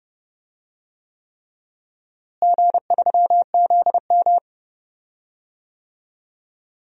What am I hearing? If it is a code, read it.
Morse code, heard as a keyed tone, about 30 wpm, 700 Hz. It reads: G3ZM
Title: DXCC-100 Call Signs - 4 Characters - Repeated 3x - 30wpm